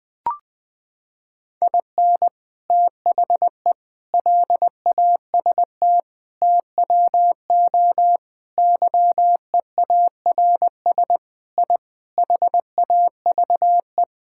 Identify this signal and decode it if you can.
Morse code, heard as a keyed tone, about 20 wpm, 700 Hz. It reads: IN THE LAST TWO YEARS I HAVE
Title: Sentences with 5-Word Phrases - 20wpm